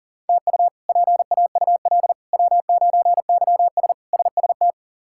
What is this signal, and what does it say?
Morse code, heard as a keyed tone, about 40 wpm, 700 Hz. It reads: TU PAUL W9YS SST
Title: K1USN SST QSO Practice - 40wpm